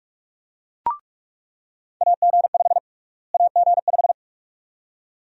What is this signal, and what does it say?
Morse code, heard as a keyed tone, about 46 wpm, 700 Hz. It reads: AG5 AG5
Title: USA Call Sign Prefixes - Repeated 2x - 45wpm